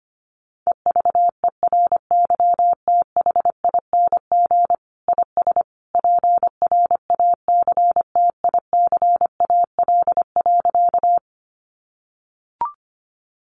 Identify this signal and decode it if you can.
Morse code, heard as a keyed tone, about 25 wpm, 700 Hz. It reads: EVERYTHING IS PRACTICAL.
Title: Sentences from Top 1400 Words 25wpm